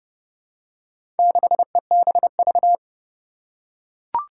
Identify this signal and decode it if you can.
Morse code, heard as a keyed tone, about 30 wpm, 700 Hz. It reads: 6EBV
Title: Four Challenging Letter-Number Combinations 30wpm